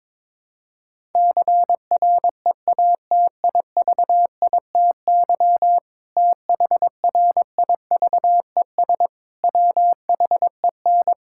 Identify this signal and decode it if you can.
Morse code, heard as a keyed tone, about 21 wpm, 700 Hz. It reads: CREATIVITY THRIVES WHEN
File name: Difficult Sentences with Hints - 22wpm